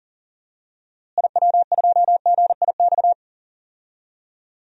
Morse code, heard as a keyed tone, about 40 wpm, 700 Hz. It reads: IW2GIX
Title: DXCC-100 Call Signs - 6 Characters - Repeated 2x - 40wpm